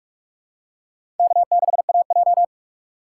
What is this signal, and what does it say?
Morse code, heard as a keyed tone, about 44 wpm, 700 Hz. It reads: K6AJ
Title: USA Call Signs - 4 Characters - 3x Character Spacing - 45wpm